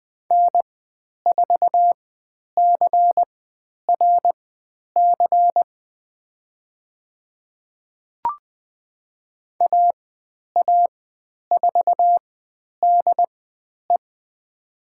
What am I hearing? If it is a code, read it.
Morse code, heard as a keyed tone, about 20 wpm, spaced as for 10 wpm, 700 Hz. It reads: N4CRC AA4DE
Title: 102 Lesson 17g - Letter u - Call Signs 20wpm